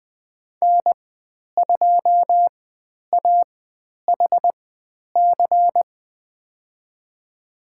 Morse code, heard as a keyed tone, about 20 wpm, spaced as for 10 wpm, 700 Hz. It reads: N2AHC
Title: 109 Lesson 18g - Letter m - Call Signs 20wpm